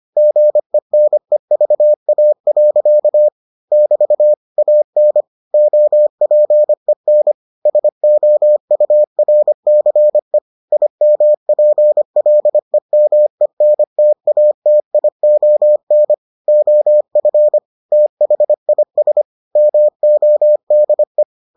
Morse code, heard as a keyed tone, about 25 wpm, 600 Hz. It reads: GENEVA. <BT> AN OPEN SOURCE IMPLEMENTATION OF THIS MODE